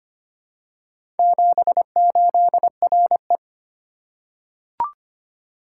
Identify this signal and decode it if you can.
Morse code, heard as a keyed tone, about 25 wpm, 700 Hz. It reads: 78RE
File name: practice / 25wpm / Four Letter-Number Combinations - 25wpm